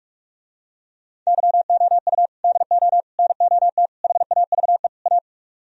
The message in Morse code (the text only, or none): YOU DO NOT HAVE A